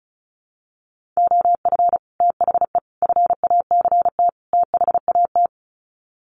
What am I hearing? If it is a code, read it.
Morse code, heard as a keyed tone, about 34 wpm, 700 Hz. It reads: OF THE FACT THAT
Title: Sets of 4 Words 35wpm